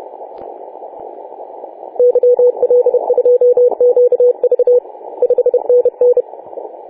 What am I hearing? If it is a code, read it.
Morse code, heard as a keyed tone, about 30 wpm, 490 Hz. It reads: YL2QV 5NN
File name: 4W/G3ZEM 28.010MHz CW